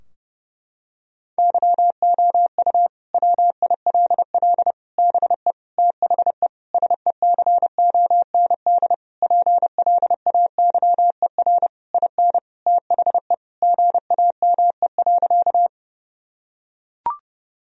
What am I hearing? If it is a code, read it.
Morse code, heard as a keyed tone, about 30 wpm, 700 Hz. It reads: YOU WILL BE THE SECOND PLAYER IN THE GAME.
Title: Sentences from Top 1000 Words - Review Part 4 - 30wpm